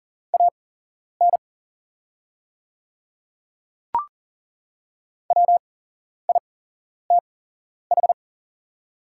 Morse code, heard as a keyed tone, about 39 wpm, spaced as for 11 wpm, 700 Hz. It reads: AN WITH
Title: 8x Character Spacing - Top 100 Words - 40wpm